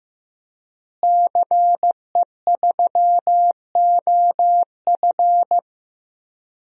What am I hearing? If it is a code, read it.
Morse code, heard as a keyed tone, about 15 wpm, 700 Hz. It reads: CE3OF
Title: DXCC-100 Call Signs - 5 Characters - Repeated 2x - 15wpm